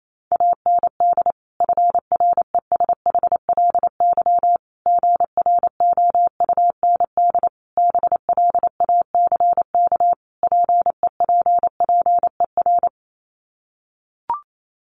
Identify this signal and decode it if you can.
Morse code, heard as a keyed tone, about 28 wpm, 700 Hz. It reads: AND FRESHLY GROUND BLACK PEPPER
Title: Sets of 5 Words Encore 28wpm